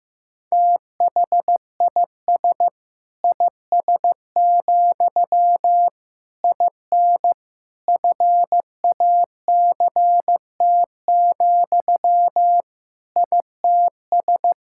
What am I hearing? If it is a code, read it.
Morse code, heard as a keyed tone, about 15 wpm, 700 Hz. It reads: THIS IS, IN FACT, ITS
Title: Sentences from Top 1000 Words - Review Part 5 - 15wpm